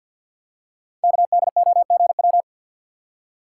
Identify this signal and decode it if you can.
Morse code, heard as a keyed tone, about 50 wpm, 700 Hz. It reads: KDOGW
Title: Code Groups - 5 Letters - 50wpm